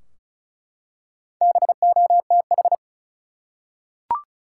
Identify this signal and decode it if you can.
Morse code, heard as a keyed tone, about 36 wpm, 700 Hz. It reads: BOTH
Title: Four Letter Words - 35wpm